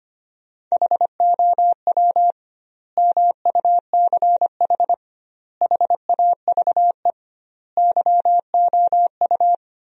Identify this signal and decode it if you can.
Morse code, heard as a keyed tone, about 25 wpm, 700 Hz. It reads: HOW MUCH HAVE YOU
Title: 2x Word-Spacing - Sentences from Top 300 Words - 25wpm